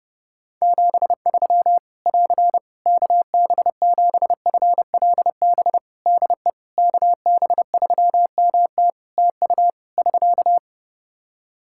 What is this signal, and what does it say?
Morse code, heard as a keyed tone, about 30 wpm, 700 Hz. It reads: 73 <AR> KB7FLB DE KB3MT TU <SK>
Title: Practice QSO - Exchange 4 - Ending - 30wpm